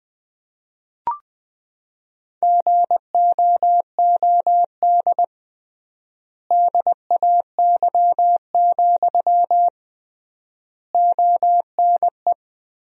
Morse code, heard as a keyed tone, about 20 wpm, 700 Hz. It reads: GOOD DAY, ONE
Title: 3x Word-Spacing - Sentences from Top 200 Words - 20wpm